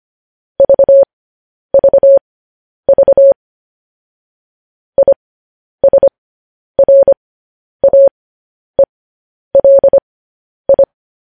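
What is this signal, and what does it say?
Morse code, heard as a keyed tone, about 25 wpm, spaced as for 10 wpm, 550 Hz. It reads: VVV ISRAELI
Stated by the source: Morse code news headlines, likely sent at 10 WPM.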